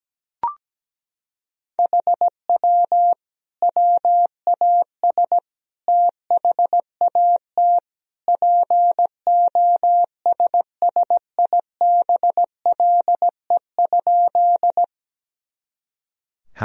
Morse code, heard as a keyed tone, about 17 wpm, 700 Hz. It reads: HW WAS THAT POSSIBLE?
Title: Sentences from Top 500 Words and Abbreviations 17wpm